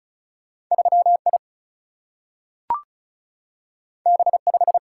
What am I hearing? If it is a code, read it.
Morse code, heard as a keyed tone, about 36 wpm, 700 Hz. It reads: 3 I B 5
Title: Single-Letter Single-Number 35wpm